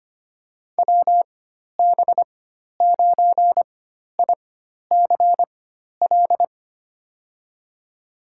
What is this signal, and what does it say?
Morse code, heard as a keyed tone, about 25 wpm, spaced as for 12 wpm, 700 Hz. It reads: WB9ICL
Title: USA Call Signs - 6 Characters - 4x Character Spacing - 25wpm